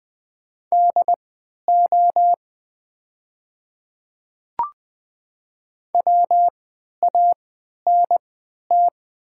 Morse code, heard as a keyed tone, about 20 wpm, spaced as for 11 wpm, 700 Hz. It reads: DO WANT
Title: 3x Character Spacing - Top 200 Words - 20wpm